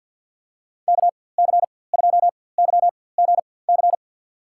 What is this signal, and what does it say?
Morse code, heard as a keyed tone, about 52 wpm, spaced as for 24 wpm, 700 Hz. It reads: KC2YGC
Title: USA Call Signs - 6 Characters - 4x Character Spacing - 50wpm